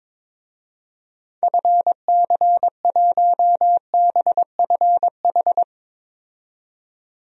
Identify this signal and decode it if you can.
Morse code, heard as a keyed tone, about 22 wpm, 700 Hz. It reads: FC1BFH
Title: DXCC-100 Call Signs - 6 Characters - 22wpm